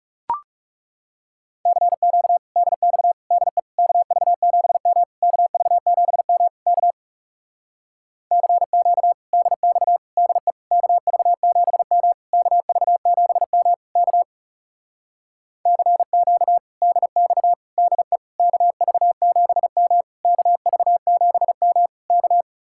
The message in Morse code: CQ DX DE KV7M KV7M K CQ DX DE KV7M KV7M K CQ DX DE KV7M KV7M K

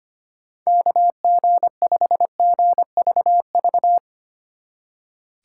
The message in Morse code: KG5GVV

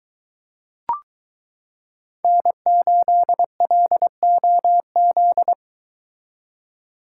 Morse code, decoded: N8LOZ

